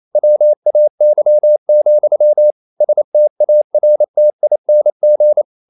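Morse code, heard as a keyed tone, about 28 wpm, 600 Hz. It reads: WAY, STARTING
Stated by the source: Morse code practice at 28 wpm, text of Ken Follett's novel Never in English, from CW med Gnister.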